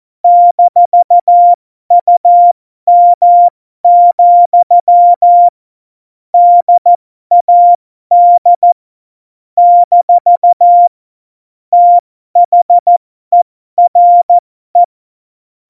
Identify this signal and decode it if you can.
Morse code, heard as a keyed tone, about 14 wpm, spaced as for 12 wpm, 700 Hz. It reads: - UM, DAD - THERE
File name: Morsekurs / Witze